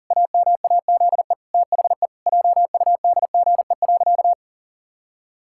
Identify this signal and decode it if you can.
Morse code, heard as a keyed tone, about 40 wpm, 700 Hz. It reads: AMAZE THE JUDGE.
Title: Pangram Challenge v2 - 40wpm